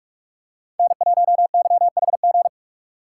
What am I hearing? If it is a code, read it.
Morse code, heard as a keyed tone, about 42 wpm, 700 Hz. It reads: N1YHG
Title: USA Call Signs - 5 Characters - 3x Character Spacing - 45wpm